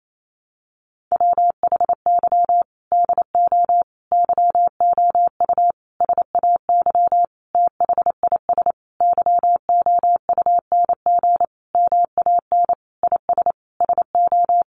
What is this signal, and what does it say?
Morse code, heard as a keyed tone, about 28 wpm, 700 Hz. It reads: WHY DO YOU SAY THIS YOUNG MAN IS SO
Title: Sentences from Top 1100 Words 28wpm